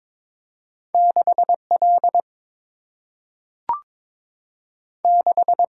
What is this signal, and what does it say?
Morse code, heard as a keyed tone, about 22 wpm, 700 Hz. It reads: 6 L 6
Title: Single-Letter Single-Number 22wpm